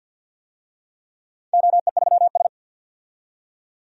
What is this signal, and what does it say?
Morse code, heard as a keyed tone, about 53 wpm, 700 Hz. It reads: OE3S
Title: DXCC-100 Call Signs - 4 Characters - 50wpm